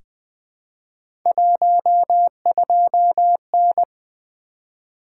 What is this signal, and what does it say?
Morse code, heard as a keyed tone, about 20 wpm, 700 Hz. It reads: 12N
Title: Three Letter-Number Combinations - 20wpm